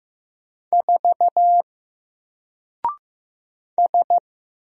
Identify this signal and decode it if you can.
Morse code, heard as a keyed tone, about 15 wpm, 700 Hz. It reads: 4 S
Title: Single Letter-Number 15wpm